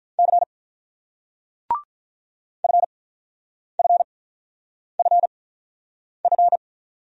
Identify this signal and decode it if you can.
Morse code, heard as a keyed tone, about 47 wpm, spaced as for 9 wpm, 700 Hz. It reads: C FFFF